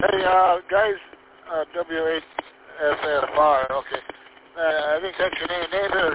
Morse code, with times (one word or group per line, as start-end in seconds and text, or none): none